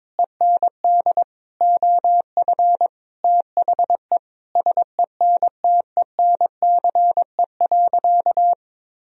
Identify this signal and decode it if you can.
Morse code, heard as a keyed tone, about 21 wpm, 700 Hz. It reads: END OF THE SENTENCE.
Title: Sentences from Top 1000 Words - Review Part 1 - 22wpm